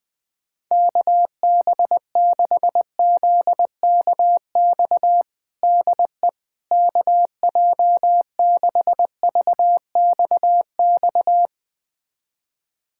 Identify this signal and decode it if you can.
Morse code, heard as a keyed tone, about 20 wpm, 700 Hz. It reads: KB6ZKX DE KJ6VXX
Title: USA Callsign DE Callsign - 6 Characters - 20wpm